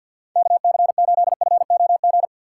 Morse code, heard as a keyed tone, about 48 wpm, 700 Hz. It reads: KC8FOG
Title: USA Call Signs - 6 Characters - 3x Character Spacing - 50wpm